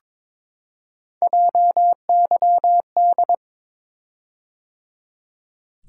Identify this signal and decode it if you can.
Morse code, heard as a keyed tone, about 22 wpm, 700 Hz. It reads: JYD
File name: practice / 22wpm / Three Letter-Number Combinations - 22wpm